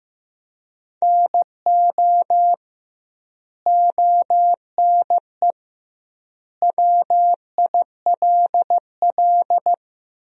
Morse code, heard as a keyed tone, about 16 wpm, 700 Hz. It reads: NO ONE WILL
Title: 2x Word-Spacing - Sentences from Top 100 Words - 15wpm